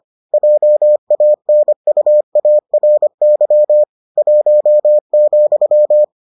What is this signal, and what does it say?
Morse code, heard as a keyed tone, about 25 wpm, 600 Hz. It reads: JANUARY 1,